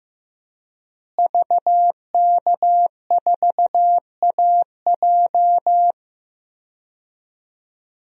Morse code, heard as a keyed tone, about 16 wpm, 700 Hz. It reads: VK4AJ
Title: DXCC-100 Call Signs - 5 Characters - Repeated 2x - 15wpm